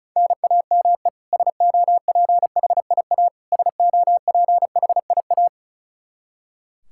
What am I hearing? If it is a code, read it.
Morse code, heard as a keyed tone, about 36 wpm, 700 Hz. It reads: NAME SOPHIA SOPHIA
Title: Practice QSO - Exchange 1 - Beginning - 35wpm